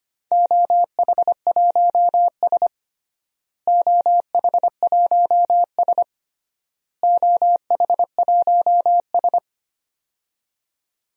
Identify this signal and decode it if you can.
Morse code, heard as a keyed tone, about 25 wpm, 700 Hz. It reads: OH1S OH1S OH1S
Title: DXCC-100 Call Signs - 4 Characters - Repeated 3x - 25wpm